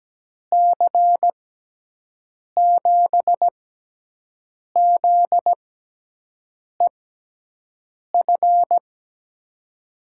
Morse code, heard as a keyed tone, about 17 wpm, 700 Hz. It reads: C 7 Z E F
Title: USA Call Signs - 6 Characters - 6x Character Spacing - 17wpm